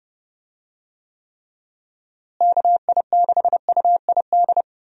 Morse code, heard as a keyed tone, about 30 wpm, 700 Hz. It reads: KI6UID
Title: USA Call Signs - 6 Characters - Repeated 2x - 30wpm